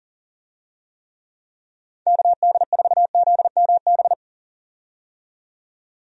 Morse code, heard as a keyed tone, about 40 wpm, 700 Hz. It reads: KD4ZMB
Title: USA Call Signs - 6 Characters - 40wpm